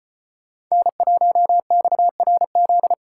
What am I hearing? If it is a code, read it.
Morse code, heard as a keyed tone, about 33 wpm, 700 Hz. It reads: N1XRZ